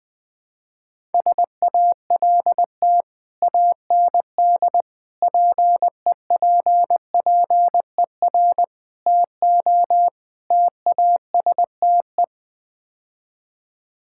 Morse code, heard as a keyed tone, about 20 wpm, 700 Hz. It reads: SALT AND PEPPER TO TASTE
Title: Build A Phrase - 5-Word Phrases - 20wpm